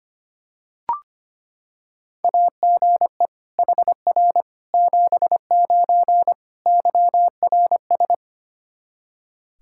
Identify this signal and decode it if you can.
Morse code, heard as a keyed tone, about 25 wpm, 700 Hz. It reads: AGE HR 79 YRS
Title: K1USN SST QSO Practice 25wpm